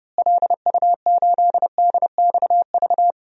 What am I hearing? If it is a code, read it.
Morse code, heard as a keyed tone, about 29 wpm, 700 Hz. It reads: LU8DXV
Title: DXCC-100 Call Signs - 6 Characters - Repeated 2x - 30wpm